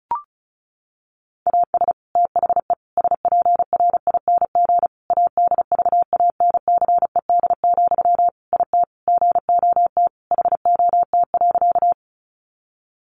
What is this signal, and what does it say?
Morse code, heard as a keyed tone, about 34 wpm, 700 Hz. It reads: AS THE SPRING ADVANCED, IT GOT HOT.
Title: Sentences from Top 1200 Words 35wpm